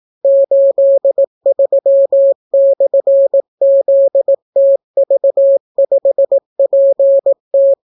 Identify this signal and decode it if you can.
Morse code, heard as a keyed tone, about 18 wpm, 550 Hz. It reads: 83/ZTV5PT